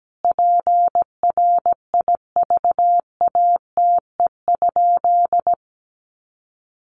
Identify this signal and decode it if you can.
Morse code, heard as a keyed tone, about 17 wpm, 700 Hz. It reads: PRIVATE?
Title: Sentences from Top 1000 Words 17wpm